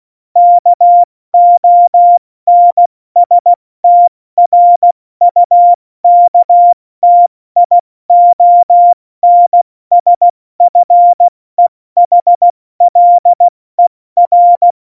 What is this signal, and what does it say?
Morse code, heard as a keyed tone, about 16 wpm, 700 Hz. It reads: KONSTRUKTIONSFEHLER
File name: Morsekurs / Witze / Witz 26. Januar 2021, 70 F80